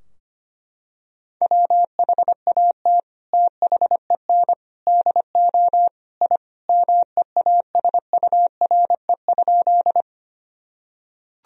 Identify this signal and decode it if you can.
Morse code, heard as a keyed tone, about 25 wpm, 700 Hz. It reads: WHAT THEN DO I MEASURE?
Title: Sentences from Top 1000 Words - Review Part 2 - 25wpm